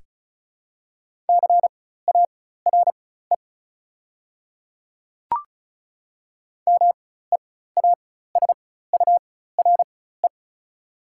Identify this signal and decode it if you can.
Morse code, heard as a keyed tone, about 36 wpm, spaced as for 16 wpm, 700 Hz. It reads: CARE MEASURE
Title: 4x Character Spacing - Top 500 Words - 35wpm